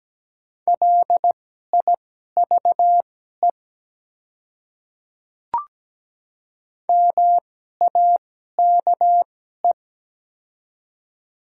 Morse code, heard as a keyed tone, about 17 wpm, spaced as for 12 wpm, 700 Hz. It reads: LIVE MAKE
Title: 2x Character Spacing - Top 300 Words - 17wpm